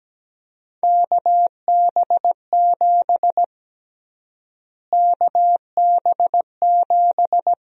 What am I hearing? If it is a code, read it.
Morse code, heard as a keyed tone, about 17 wpm, 700 Hz. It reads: KB7 KB7
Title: USA Call Sign Prefixes - Repeated 2x - 17wpm